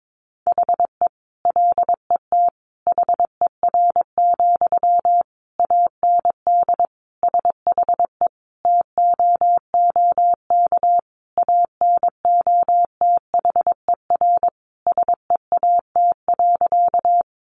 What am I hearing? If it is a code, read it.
Morse code, heard as a keyed tone, about 22 wpm, 700 Hz. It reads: HE LET HER, AND SHE TOOK ANOTHER SEAT.
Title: Sentences from Top 1500 Words 22wpm